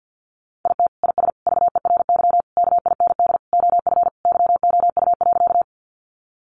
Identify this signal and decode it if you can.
Morse code, heard as a keyed tone, about 50 wpm, 700 Hz. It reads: IT IS VERY KIND OF YOU.
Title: Sentences from Top 200 Words 50wpm